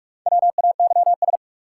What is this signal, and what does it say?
Morse code, heard as a keyed tone, about 46 wpm, 700 Hz. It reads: WAYS
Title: Sets of 5 Words Encore - New Word Drill - 45wpm